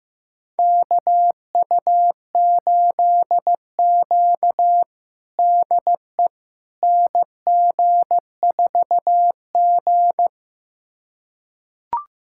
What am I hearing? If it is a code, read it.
Morse code, heard as a keyed tone, about 15 wpm, 700 Hz. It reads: KU8Q DE NG4G
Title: USA Callsign DE Callsign - 4 Characters - 15wpm